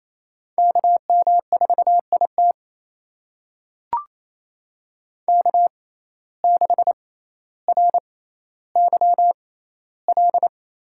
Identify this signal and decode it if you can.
Morse code, heard as a keyed tone, about 28 wpm, spaced as for 14 wpm, 700 Hz. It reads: KM4IT K6RYL
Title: USA Call Signs - 5 Characters - 6x Character Spacing - 28wpm